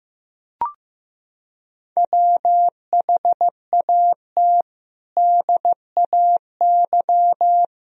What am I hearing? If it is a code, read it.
Morse code, heard as a keyed tone, about 14 wpm, 700 Hz. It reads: WHAT DAY
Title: Sentences from Top 1000 Words - Review Part 3 - 15wpm